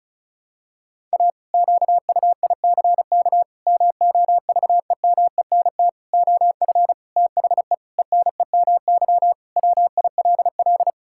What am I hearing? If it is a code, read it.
Morse code, heard as a keyed tone, about 34 wpm, 700 Hz. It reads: A QUICK MOVEMENT OF THE ENEMY WILL
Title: Pangram Challenge v2 - 35wpm